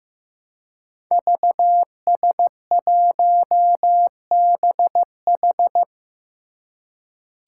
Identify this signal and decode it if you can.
Morse code, heard as a keyed tone, about 15 wpm, 700 Hz. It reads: VS1BH
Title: Five Challenging Letter-Number Combinations 15wpm